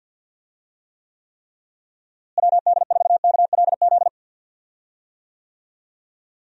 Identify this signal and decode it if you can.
Morse code, heard as a keyed tone, about 50 wpm, 700 Hz. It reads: WD4KLZ
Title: USA Call Signs - 6 Characters - Repeated 3x - 50wpm